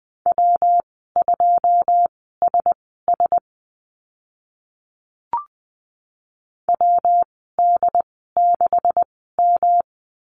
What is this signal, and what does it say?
Morse code, heard as a keyed tone, about 20 wpm, spaced as for 14 wpm, 700 Hz. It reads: W2SS WD6M
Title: 154 Lesson 25g - Letter p - Call Signs 20wpm